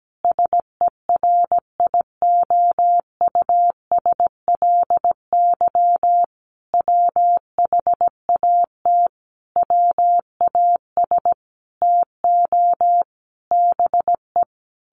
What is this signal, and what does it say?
Morse code, heard as a keyed tone, about 17 wpm, 700 Hz. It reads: SERIOUSLY WHAT WAS TO BE
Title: Sentences from Top 1200 Words 17wpm